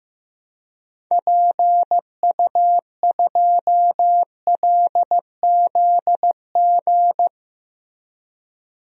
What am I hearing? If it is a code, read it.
Morse code, heard as a keyed tone, about 15 wpm, 700 Hz. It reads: PU2LZG
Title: DXCC-100 Call Signs - 6 Characters - 15wpm